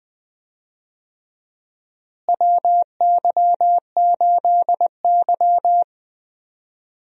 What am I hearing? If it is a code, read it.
Morse code, heard as a keyed tone, about 19 wpm, 700 Hz. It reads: WY8Y
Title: USA Call Signs - 4 Characters - Repeated 3x - 20wpm